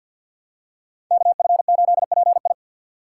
Morse code, heard as a keyed tone, about 47 wpm, 700 Hz. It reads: KF8PI
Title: USA Call Signs - 5 Characters - 4x Character Spacing - 50wpm